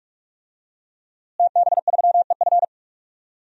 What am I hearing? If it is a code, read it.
Morse code, heard as a keyed tone, about 41 wpm, 700 Hz. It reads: TB3EF
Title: DXCC-100 Call Signs - 5 Characters - Repeated 3x - 45wpm